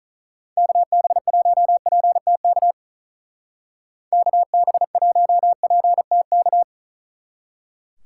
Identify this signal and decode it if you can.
Morse code, heard as a keyed tone, about 39 wpm, 700 Hz. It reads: KB1PTK KB1PTK
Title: Speed Racing - USA 2x3 Call Signs 35wpm